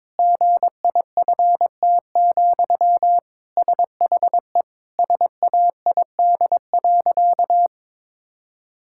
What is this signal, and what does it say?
Morse code, heard as a keyed tone, about 22 wpm, 700 Hz. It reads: GIFT, SHE SAID.
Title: Sentences from Top 1000 Words - Review Part 6 - 22wpm